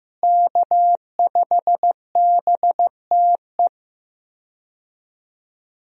Morse code, heard as a keyed tone, about 15 wpm, 700 Hz. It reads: K5BTE
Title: USA Call Signs - 5 Characters - Repeated 3x - 15wpm